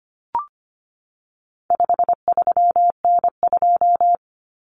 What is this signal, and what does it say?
Morse code, heard as a keyed tone, about 25 wpm, 700 Hz. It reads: 53N2
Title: Common Cut Numbers - 4-Digit 25wpm